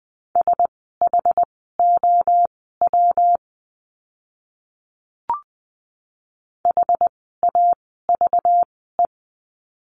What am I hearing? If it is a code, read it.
Morse code, heard as a keyed tone, about 20 wpm, spaced as for 14 wpm, 700 Hz. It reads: SHOW HAVE
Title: 244 Lesson 39f - Letter z - Words 20wpm